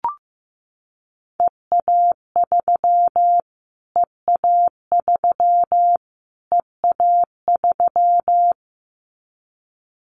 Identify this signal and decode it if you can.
Morse code, heard as a keyed tone, about 15 wpm, 700 Hz. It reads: EA3 EA3 EA3
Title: Callsign Prefixes for Top 25 DXCC - Repeated 3x - 15wpm